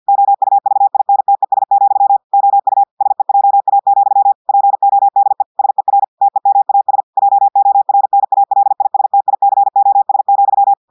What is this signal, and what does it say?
Morse code, heard as a keyed tone, about 50 wpm, 800 Hz. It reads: OUVINTES, OU SEJA, PODE SER TEMAS JORNALISTICOS,
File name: portugais texte en clair 2 vitesse 50 mots minutes